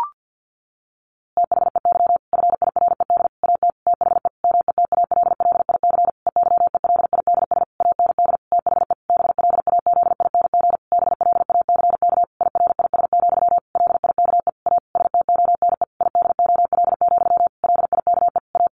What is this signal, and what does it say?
Morse code, heard as a keyed tone, about 50 wpm, 700 Hz. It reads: THEY FIRED AT THE METALLIC EYELIDS AND THE BLAZING BLACK IRIS, LIKE A STONE IDOL, LIKE A
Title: A Sound Of Thunder 50wpm